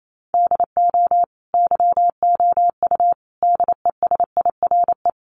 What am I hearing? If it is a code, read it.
Morse code, heard as a keyed tone, about 28 wpm, 700 Hz. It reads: DO YOU DESIRE
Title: Sentences from Top 1100 Words 28wpm